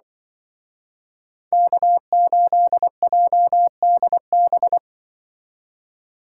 Morse code, heard as a keyed tone, about 25 wpm, 700 Hz. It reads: K8JDB
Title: Speed Racing - USA 1x3 Call Signs 20wpm